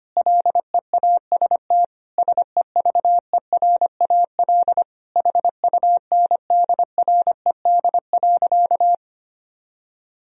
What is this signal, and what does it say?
Morse code, heard as a keyed tone, about 25 wpm, 700 Hz. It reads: LEAST SEVERAL HUNDRED.
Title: Sentences from Top 1000 Words - Review Part 2 - 25wpm